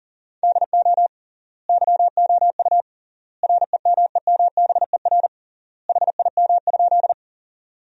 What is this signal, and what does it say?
Morse code, heard as a keyed tone, about 40 wpm, 700 Hz. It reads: DO YOU REMEMBER HIM?
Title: 3x Word-Spacing - Sentences from Top 400 Words - 40wpm